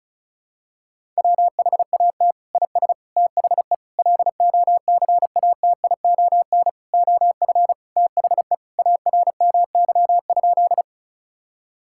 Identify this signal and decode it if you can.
Morse code, heard as a keyed tone, about 36 wpm, 700 Hz. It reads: WHAT IS THE LOCATION OF THE ARMY?
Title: Sentences from Top 1000 Words - Review Part 6 - 35wpm